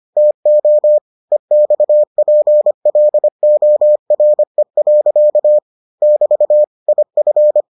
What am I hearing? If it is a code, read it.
Morse code, heard as a keyed tone, about 25 wpm, 600 Hz. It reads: TO EXPLORE. <BT> IF